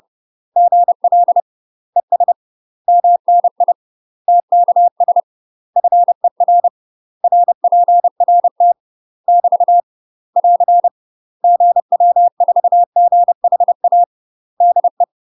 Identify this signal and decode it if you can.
Morse code, heard as a keyed tone, about 30 wpm, 700 Hz. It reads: GL ES MNI TKS FER RPRT <BT> <AR> GW4GHA DE
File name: UK MASTER QSOs TRAINING / 30WPM / UK MASTER QSOs 0043